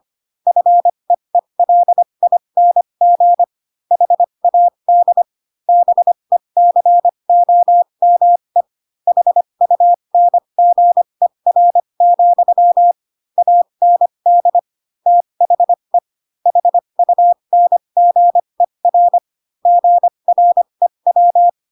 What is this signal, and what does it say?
Morse code, heard as a keyed tone, about 25 wpm, spaced as for 22 wpm, 700 Hz. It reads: FEELING HAD BECOME HUNGER, AND THE HUNGER GREW